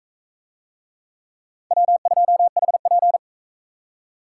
Morse code, heard as a keyed tone, about 41 wpm, 700 Hz. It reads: W2HP